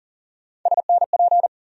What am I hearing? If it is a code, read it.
Morse code, heard as a keyed tone, about 40 wpm, 700 Hz. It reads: SNP
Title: USA Call Sign Postfixes - 3 Letters 40wpm